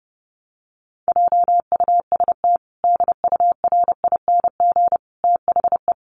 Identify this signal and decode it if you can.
Morse code, heard as a keyed tone, about 28 wpm, 700 Hz. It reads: JUST DURING THE
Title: Sentences from Top 700 Words 30wpm